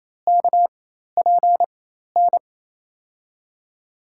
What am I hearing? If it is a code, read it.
Morse code, heard as a keyed tone, about 28 wpm, spaced as for 13 wpm, 700 Hz. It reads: KPN
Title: USA Call Signs - 5 Characters - 4x Character Spacing - 28wpm